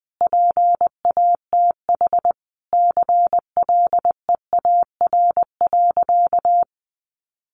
Morse code, heard as a keyed tone, about 20 wpm, 700 Hz. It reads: PATH CLEAR.
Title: Sentences from Top 1400 Words 20wpm